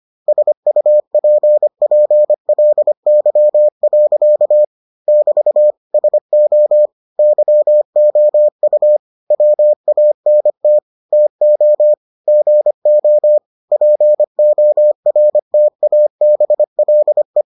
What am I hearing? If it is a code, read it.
Morse code, heard as a keyed tone, about 25 wpm, 600 Hz. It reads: SUPPLY. <BT> SO YOU WANT TO GO PORTABLE